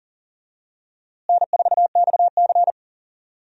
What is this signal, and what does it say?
Morse code, heard as a keyed tone, about 39 wpm, 700 Hz. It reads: N4XC
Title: USA Call Signs - 4 Characters - Repeated 3x - 40wpm